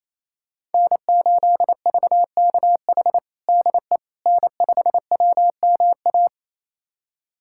Morse code, heard as a keyed tone, about 28 wpm, 700 Hz. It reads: N8VKH DE N5WMA
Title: USA Callsign DE Callsign - 5 Characters - 28wpm